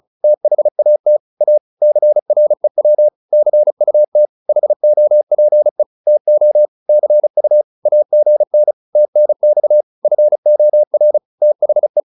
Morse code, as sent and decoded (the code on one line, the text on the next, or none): - .... .- - / .- / -.-. .-. .-- / -.-. ..- - / .... --- .--. / - --- / -.-. ..- / .- --. -. / - -. -..- / ..-. --- .-. / - .... .
THAT A CREW CUT HOPE TO CU AGN TNX FOR THE